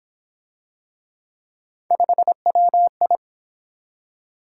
Morse code, heard as a keyed tone, about 29 wpm, 700 Hz. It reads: 5WI